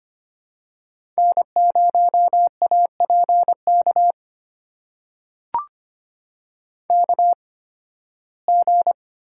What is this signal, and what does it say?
Morse code, heard as a keyed tone, about 25 wpm, spaced as for 16 wpm, 700 Hz. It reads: N0APK KG
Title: USA Call Signs - 5 Characters - 8x Character Spacing - 25wpm